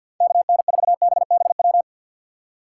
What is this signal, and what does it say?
Morse code, heard as a keyed tone, about 65 wpm, 700 Hz. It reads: KN4BBW